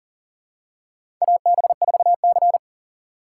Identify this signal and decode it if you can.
Morse code, heard as a keyed tone, about 40 wpm, 700 Hz. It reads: AB4C
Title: USA Call Signs - 4x Character Spacing - 40wpm